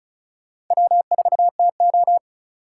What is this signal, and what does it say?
Morse code, heard as a keyed tone, about 45 wpm, 700 Hz. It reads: W4TO